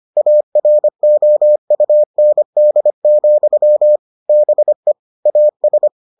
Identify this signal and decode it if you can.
Morse code, heard as a keyed tone, about 25 wpm, 600 Hz. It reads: AROUND, BE AS